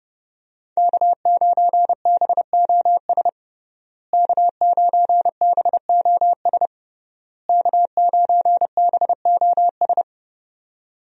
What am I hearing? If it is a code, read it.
Morse code, heard as a keyed tone, about 30 wpm, 700 Hz. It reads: K9BOS K9BOS K9BOS
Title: USA Call Signs - 5 Characters - Repeated 3x - 30wpm